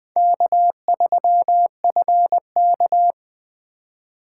Morse code, heard as a keyed tone, about 19 wpm, 700 Hz. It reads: K3FK